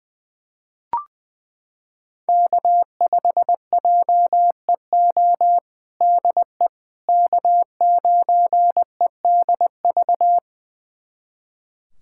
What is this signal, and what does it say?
Morse code, heard as a keyed tone, about 20 wpm, 700 Hz. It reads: K5JEO DE K9EDV
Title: USA Callsign DE Callsign - 5 Characters - 20wpm